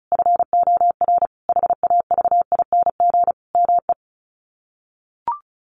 Morse code, heard as a keyed tone, about 34 wpm, 700 Hz. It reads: FOR HAVING ME
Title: Sets of 5 Words Encore 35wpm